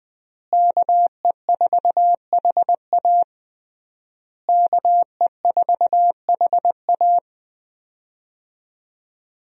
Morse code, heard as a keyed tone, about 20 wpm, 700 Hz. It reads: KE4HA KE4HA
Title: USA Call Signs - 5 Characters - Repeated 3x - 20wpm